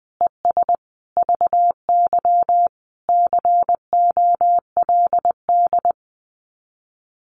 Morse code, heard as a keyed tone, about 20 wpm, 700 Hz. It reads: ES VY COLD
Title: Common QSO Phrases 20wpm